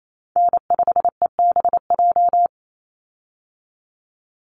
Morse code, heard as a keyed tone, about 28 wpm, 700 Hz. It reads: N5EBJ
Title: USA Call Signs 28wpm